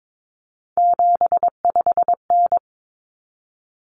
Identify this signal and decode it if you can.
Morse code, heard as a keyed tone, about 22 wpm, 700 Hz. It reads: 75N
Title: Common Cut Numbers - 3-Digit 22wpm